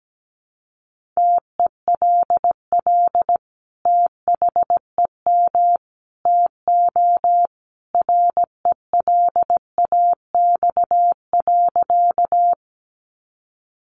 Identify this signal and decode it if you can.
Morse code, heard as a keyed tone, about 17 wpm, 700 Hz. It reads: TELL THEM TO RELAX.
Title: Sentences from Top 1400 Words 17wpm